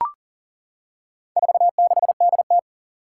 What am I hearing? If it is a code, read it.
Morse code, heard as a keyed tone, about 40 wpm, 700 Hz. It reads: 46DT
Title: Four Challenging Letter-Number Combinations 40wpm